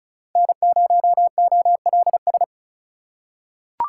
Morse code, heard as a keyed tone, about 34 wpm, 700 Hz. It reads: N0OLS
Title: USA Call Signs - 5 Characters - 3x Character Spacing - 35wpm